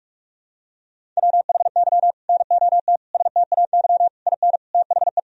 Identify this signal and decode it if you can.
Morse code, heard as a keyed tone, about 46 wpm, 700 Hz. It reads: WHY NOT STAY IN THE
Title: Sentences from Top 1000 Words - Review Part 1 - 45wpm